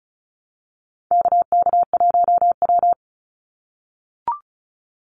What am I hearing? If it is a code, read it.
Morse code, heard as a keyed tone, about 34 wpm, 700 Hz. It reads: KK1W
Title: NA SOTA Activator Callsigns 35wpm